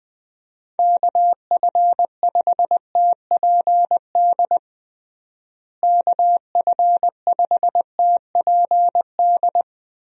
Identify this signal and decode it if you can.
Morse code, heard as a keyed tone, about 20 wpm, 700 Hz. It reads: KF5TPD KF5TPD
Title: USA Call Signs - 6 Characters - Repeated 2x - 20wpm